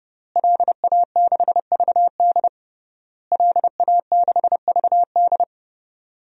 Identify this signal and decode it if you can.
Morse code, heard as a keyed tone, about 30 wpm, 700 Hz. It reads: LA6VD LA6VD
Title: DXCC-100 Call Signs - 5 Characters - Repeated 3x - 30wpm